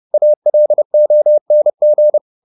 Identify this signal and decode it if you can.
Morse code, heard as a keyed tone, about 30 wpm, 600 Hz. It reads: ALONG
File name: the old man and the sea / the old man and the sea 1 30wpm